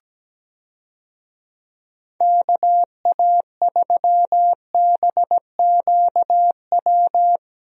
Morse code, heard as a keyed tone, about 17 wpm, 700 Hz. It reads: KA3BQW
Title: USA Call Signs - 6 Characters - 17wpm